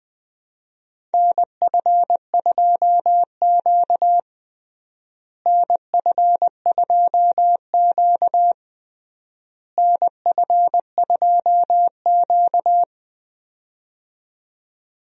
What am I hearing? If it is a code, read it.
Morse code, heard as a keyed tone, about 20 wpm, 700 Hz. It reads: NF2Q NF2Q NF2Q
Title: USA Call Signs - 4 Characters - Repeated 3x - 20wpm